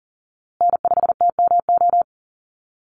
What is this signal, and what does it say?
Morse code, heard as a keyed tone, about 40 wpm, 700 Hz. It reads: N5TMO
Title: USA Call Signs Repeated 3x 40wpm